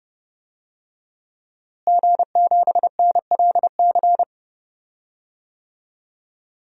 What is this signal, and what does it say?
Morse code, heard as a keyed tone, about 32 wpm, 700 Hz. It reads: G7NLC
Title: DXCC-100 Call Signs - 5 Characters - Repeated 2x - 30wpm